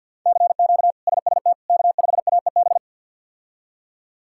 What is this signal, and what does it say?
Morse code, heard as a keyed tone, about 46 wpm, 700 Hz. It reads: CQ SST K5AEB
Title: K1USN SST QSO Practice - 50wpm